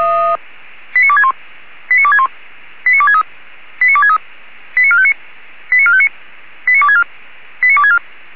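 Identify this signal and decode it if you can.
Five-tone selective call (ZVEI), heard as quick runs of five tones, tones of about 70 ms. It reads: 89261 89261 89263 89263 89459 89459 89264 89264
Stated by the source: ZVEI five-tone call 89284 received by SDR, channel 464.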